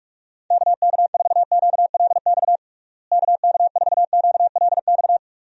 Morse code, heard as a keyed tone, about 46 wpm, 700 Hz. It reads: KK4QLX KK4QLX
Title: USA Call Signs - 6 Characters - Repeated 3x - 45wpm